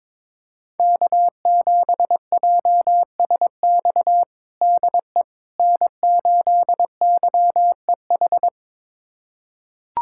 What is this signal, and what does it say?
Morse code, heard as a keyed tone, about 21 wpm, 700 Hz. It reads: K7JSX DE N8YEH
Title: USA Callsign DE Callsign - 5 Characters - 22wpm